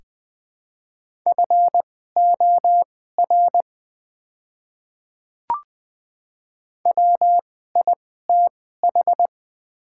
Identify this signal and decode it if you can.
Morse code, heard as a keyed tone, about 20 wpm, spaced as for 14 wpm, 700 Hz. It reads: FOR WITH
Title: 2x Character Spacing - Top 100 Words - 20wpm